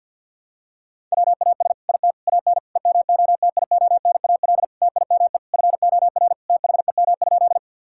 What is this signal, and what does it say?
Morse code, heard as a keyed tone, about 53 wpm, 700 Hz. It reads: WAS IT AN EMOTIONAL TIME FOR THEM?
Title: Sentences from Top 1000 Words - Review Part 5 - 50wpm